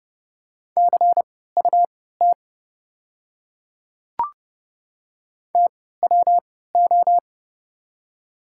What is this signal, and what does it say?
Morse code, heard as a keyed tone, about 30 wpm, spaced as for 17 wpm, 700 Hz. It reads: CUT TWO
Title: 3x Character Spacing - Top 300 Words - 30wpm